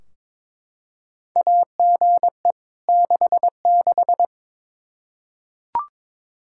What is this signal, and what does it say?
Morse code, heard as a keyed tone, about 22 wpm, 700 Hz. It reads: AGE 66
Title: Practice QSO - Exchange 3 - More Info - 22wpm